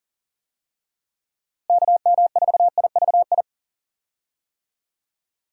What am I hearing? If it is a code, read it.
Morse code, heard as a keyed tone, about 40 wpm, 700 Hz. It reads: KM4IVI
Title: USA Call Signs - 6 Characters - Repeated 2x - 40wpm